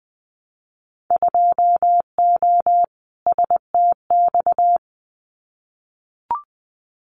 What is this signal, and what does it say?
Morse code, heard as a keyed tone, about 20 wpm, 700 Hz. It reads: 2O STX
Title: Winter Field Day Exchanges 20wpm